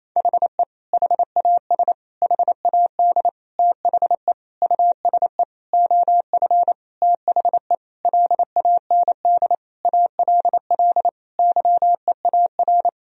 Morse code, heard as a keyed tone, about 27 wpm, 700 Hz. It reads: HE HAS HAD THE USE OF THE LAND ALL YEAR
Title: Sentences from Top 1000 Words - Review Part 1 - 28wpm